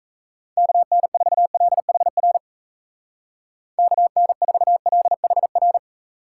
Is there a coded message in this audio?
Yes: Morse code from 0.6 to 5.8 s, heard as a keyed tone, about 37 wpm, 700 Hz. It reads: KN4LHR KN4LHR